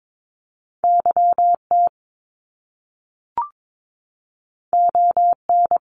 Morse code, heard as a keyed tone, about 22 wpm, 700 Hz. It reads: YT ON
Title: Canadian Provinces and Territories 22wpm